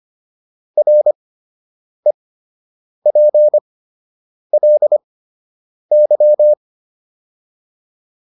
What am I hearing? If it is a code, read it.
Morse code, heard as a keyed tone, about 25 wpm, spaced as for 8 wpm, 600 Hz. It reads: REPLY